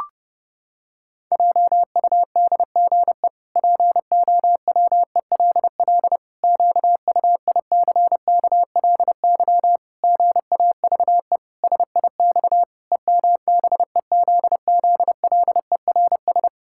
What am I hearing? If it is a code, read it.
Morse code, heard as a keyed tone, about 32 wpm, 700 Hz. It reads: JUDGE POWELL QUICKLY GAVE SIX EMBEZZLERS
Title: Pangram Challenge v2 - 30wpm